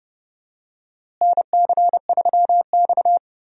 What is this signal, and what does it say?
Morse code, heard as a keyed tone, about 32 wpm, 700 Hz. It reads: NC3X
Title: USA Call Signs - 4 Characters - Repeated 2x - 30wpm